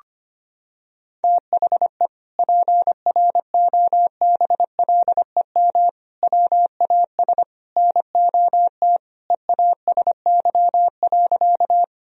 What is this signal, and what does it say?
Morse code, heard as a keyed tone, about 25 wpm, 700 Hz. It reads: THE PROBLEM WAS NOT EASY.
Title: Sentences from Top 1000 Words - Review Part 2 - 25wpm